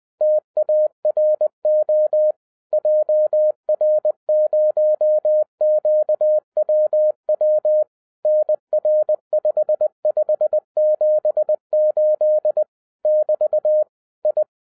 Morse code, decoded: TARO JR0QWW NR5578 <BT> I

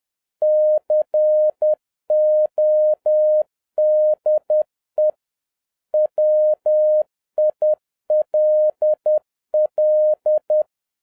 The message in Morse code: CODE WILL

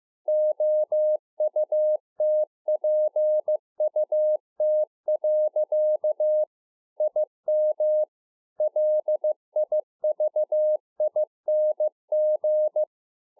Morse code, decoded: OUTPUT. IM LIVING

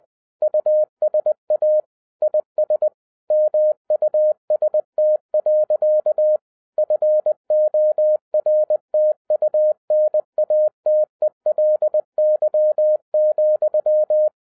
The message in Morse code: USA IS MUST. FORTUNATELY,

